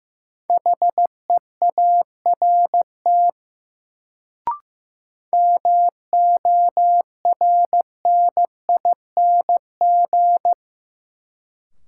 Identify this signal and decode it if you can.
Morse code, heard as a keyed tone, about 15 wpm, 700 Hz. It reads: HEART MORNING
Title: Sets of 3 Words - New Word Drill - 15wpm